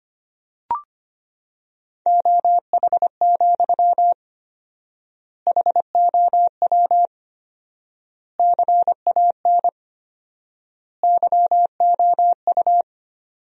Morse code, heard as a keyed tone, about 25 wpm, 700 Hz. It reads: OH, HOW CAN YOU
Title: 4x Word-Spacing - Sentences from Top 500 Words - 25wpm